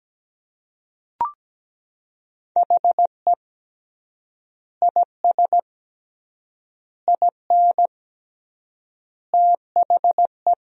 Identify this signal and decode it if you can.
Morse code, heard as a keyed tone, about 17 wpm, 700 Hz. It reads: HE IS IN THE
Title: 3x Word-Spacing - Sentences from Top 500 Words - 17wpm